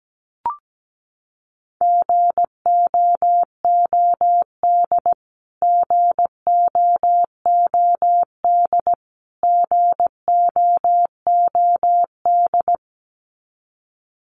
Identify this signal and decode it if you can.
Morse code, heard as a keyed tone, about 17 wpm, 700 Hz. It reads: GOOD GOOD GOOD
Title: Top 300 Words Repeated 17wpm